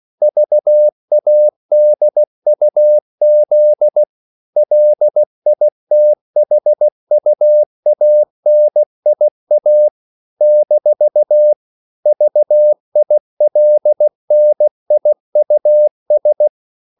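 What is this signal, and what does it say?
Morse code, heard as a keyed tone, about 16 wpm, 600 Hz. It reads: VADUZ LITHUANIA - VILNIUS
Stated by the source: CW practice of national capitals at 16 words per minute, from ON6ZQ.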